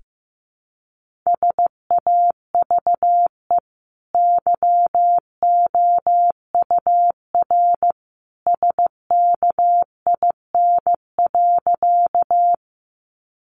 Morse code, heard as a keyed tone, about 15 wpm, 700 Hz. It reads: SAVE YOUR SKIN.
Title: Sentences from Top 1300 Words 15wpm